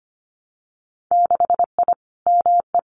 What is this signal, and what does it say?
Morse code, heard as a keyed tone, about 25 wpm, 700 Hz. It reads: 6I ME
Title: Winter Field Day Exchanges Encore 25wpm